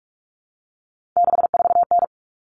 Morse code, heard as a keyed tone, about 44 wpm, 700 Hz. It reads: 64N
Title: All Cut Numbers - 3-Digit 45wpm